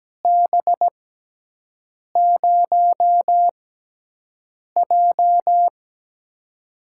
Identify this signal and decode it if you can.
Morse code, heard as a keyed tone, about 17 wpm, spaced as for 6 wpm, 700 Hz. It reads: B0J
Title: USA Call Signs - 4 Characters - 6x Character Spacing - 17wpm